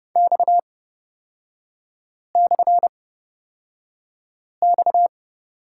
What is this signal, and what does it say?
Morse code, heard as a keyed tone, about 30 wpm, spaced as for 5 wpm, 700 Hz. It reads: X/X
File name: Lesson 37e - Letter x - Three Characters 30wpm